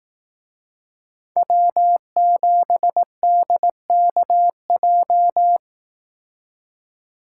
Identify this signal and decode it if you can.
Morse code, heard as a keyed tone, about 18 wpm, 700 Hz. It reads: W7DKJ
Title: Speed Racing - USA 1x3 Call Signs 15wpm